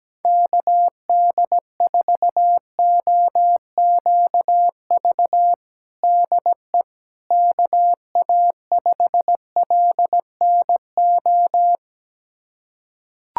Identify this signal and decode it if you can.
Morse code, heard as a keyed tone, about 17 wpm, 700 Hz. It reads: KD4OQV DE KA5LNO
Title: USA Callsign DE Callsign - 6 Characters - 17wpm